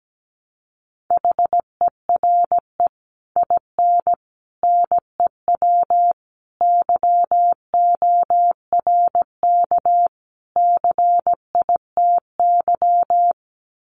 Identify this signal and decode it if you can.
Morse code, heard as a keyed tone, about 17 wpm, 700 Hz. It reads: HERE IN NEW YORK CITY
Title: Sets of 5 Words Encore2 17wpm